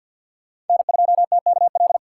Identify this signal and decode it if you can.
Morse code, heard as a keyed tone, about 46 wpm, 700 Hz. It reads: N2TKL
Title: USA Call Signs - 5 Characters - 4x Character Spacing - 50wpm